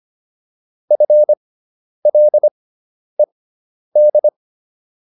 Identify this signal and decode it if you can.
Morse code, heard as a keyed tone, about 25 wpm, spaced as for 10 wpm, 600 Hz. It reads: FLED